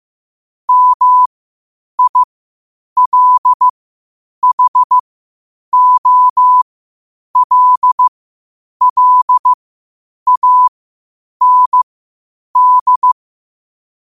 Morse code, heard as a keyed tone, about 15 wpm, spaced as for 8 wpm, 1000 Hz. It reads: MILHOLLAND
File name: qotdfiles / 8wpm